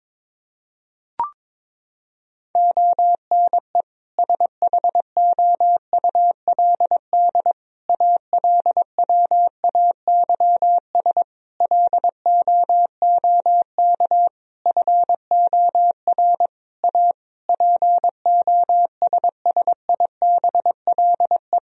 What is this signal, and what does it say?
Morse code, heard as a keyed tone, about 22 wpm, 700 Hz. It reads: ONE SHOULD ALWAYS LOOK FOR A POSSIBLE
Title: Sentences from Top 1000 Words - Review Part 4 - 22wpm